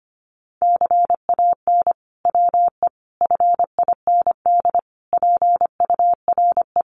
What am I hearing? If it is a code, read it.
Morse code, heard as a keyed tone, about 25 wpm, 700 Hz. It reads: CAN WE FIND PURE
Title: Sentences from Top 1500 Words 25wpm